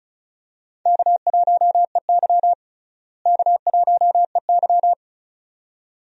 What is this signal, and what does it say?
Morse code, heard as a keyed tone, about 36 wpm, 700 Hz. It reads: K1EY K1EY
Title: USA Call Signs - 4 Characters - Repeated 2x - 35wpm